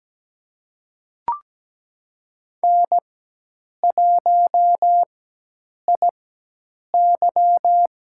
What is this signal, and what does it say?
Morse code, heard as a keyed tone, about 17 wpm, spaced as for 8 wpm, 700 Hz. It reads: N1IY
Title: USA Call Signs - 4x Character Spacing - 17wpm